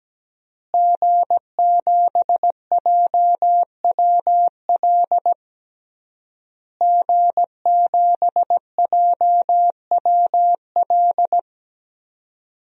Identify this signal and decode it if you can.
Morse code, heard as a keyed tone, about 17 wpm, 700 Hz. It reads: G7JWL G7JWL
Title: DXCC-100 Call Signs - 5 Characters - Repeated 3x - 17wpm